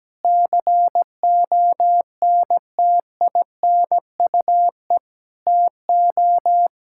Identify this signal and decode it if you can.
Morse code, heard as a keyed tone, about 17 wpm, 700 Hz. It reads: CONTINUE TO
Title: Sentences from Top 1000 Words - Review Part 6 - 17wpm